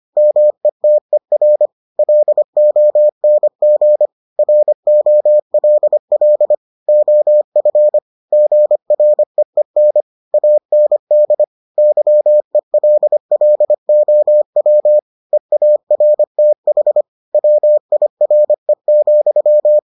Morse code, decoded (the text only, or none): METER LONG ROLL OF GREEN AND YELLOW EARTH WIRE,